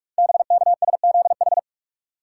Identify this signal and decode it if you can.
Morse code, heard as a keyed tone, about 44 wpm, 690 Hz. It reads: BKSZH
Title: Code Groups - 5 Letters - 45wpm